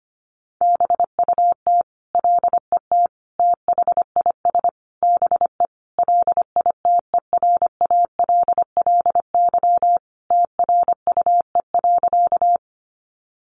Morse code, heard as a keyed tone, about 25 wpm, 700 Hz. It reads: BUT LET THIS BE LITERALLY TRUE.
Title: Sentences from Top 1400 Words 25wpm